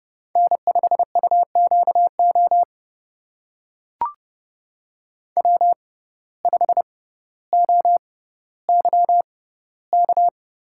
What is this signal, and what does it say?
Morse code, heard as a keyed tone, about 31 wpm, spaced as for 15 wpm, 700 Hz. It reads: N5UQO W5OYK
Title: USA Call Signs - 5 Characters - 6x Character Spacing - 30wpm